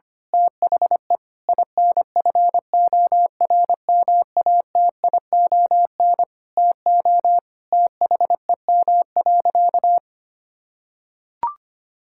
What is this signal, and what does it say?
Morse code, heard as a keyed tone, about 25 wpm, 700 Hz. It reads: THE INFORMATION TO THEM.
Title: Sentences from Top 1000 Words - Review Part 5 - 25wpm